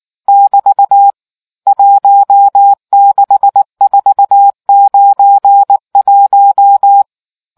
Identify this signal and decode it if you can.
Morse code, heard as a keyed tone, about 19 wpm, 800 Hz. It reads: <BT> 16491